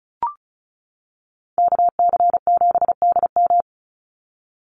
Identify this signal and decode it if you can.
Morse code, heard as a keyed tone, about 34 wpm, 700 Hz. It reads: KC7DM
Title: NA SOTA Activator Callsigns 35wpm